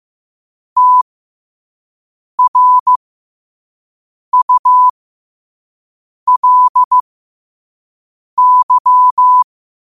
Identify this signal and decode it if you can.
Morse code, heard as a keyed tone, about 15 wpm, spaced as for 5 wpm, 1000 Hz. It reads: TRULY
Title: 5 WPM morse code quotes for Fri, 01 Mar 2024